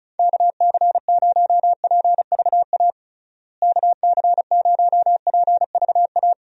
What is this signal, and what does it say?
Morse code, heard as a keyed tone, about 36 wpm, 700 Hz. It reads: KC0PVA KC0PVA
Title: USA Call Signs - 6 Characters - Repeated 2x - 35wpm